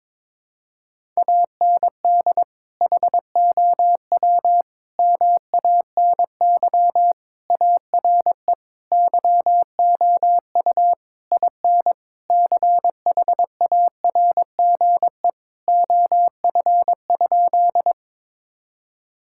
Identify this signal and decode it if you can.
Morse code, heard as a keyed tone, about 22 wpm, 700 Hz. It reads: AND HOW MANY ARE YOU IN CHARGE OF?
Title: Sentences from Top 1000 Words - Review Part 6 - 22wpm